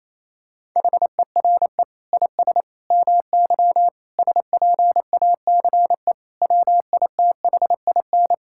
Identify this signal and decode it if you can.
Morse code, heard as a keyed tone, about 28 wpm, 700 Hz. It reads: HERE IS MY SPACE WITHIN
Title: Sentences from Top 1000 Words - Review Part 5 - 28wpm